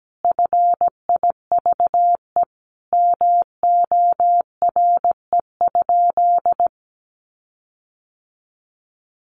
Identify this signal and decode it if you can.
Morse code, heard as a keyed tone, about 17 wpm, 700 Hz. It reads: FIVE MORE?
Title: Sentences from Top 1400 Words 17wpm